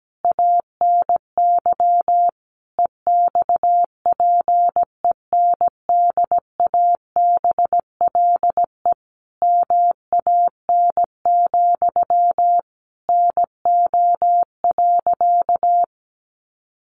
Morse code, heard as a keyed tone, about 17 wpm, 700 Hz. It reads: ANY EXPENDABLE MAN, NO.
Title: A Sound Of Thunder 17wpm